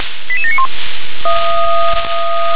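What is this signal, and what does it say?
Five-tone selective call (ZVEI), heard as quick runs of five tones, tones of about 70 ms. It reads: 88791
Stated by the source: ZVEI five-tone call 88765 received by SDR.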